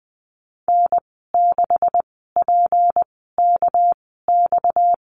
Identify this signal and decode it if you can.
Morse code, heard as a keyed tone, about 20 wpm, spaced as for 14 wpm, 700 Hz. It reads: N6PKX
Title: 238 Lesson 38g - Letter q - Call Signs 20wpm